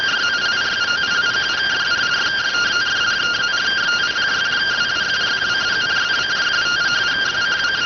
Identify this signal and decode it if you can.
Two-tone teleprinter signal, about 170 Hz shift.